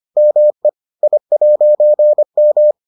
Morse code, heard as a keyed tone, about 25 wpm, 600 Hz. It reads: ME I'M